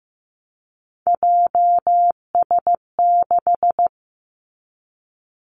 Morse code, heard as a keyed tone, about 15 wpm, 700 Hz. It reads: JS6
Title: Three Challenging Letter-Number Combinations 15wpm